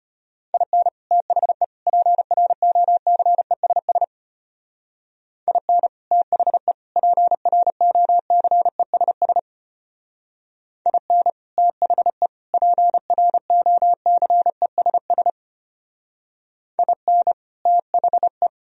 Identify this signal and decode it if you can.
Morse code, heard as a keyed tone, about 31 wpm, 700 Hz. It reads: IN THE PROCESS IN THE PROCESS IN THE PROCESS IN THE
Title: Speed-Racing - Sets of 3 Words 25wpm